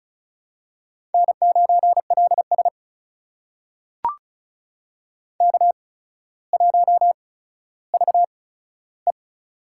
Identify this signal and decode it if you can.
Morse code, heard as a keyed tone, about 36 wpm, spaced as for 15 wpm, 700 Hz. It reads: N9LS K1VE
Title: USA Call Signs - 4 Characters - 8x Character Spacing - 35wpm